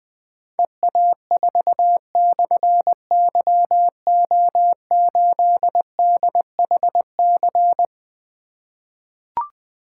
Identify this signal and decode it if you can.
Morse code, heard as a keyed tone, about 20 wpm, 700 Hz. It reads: EA4/YO8DHC
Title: Difficult Call Signs 20wpm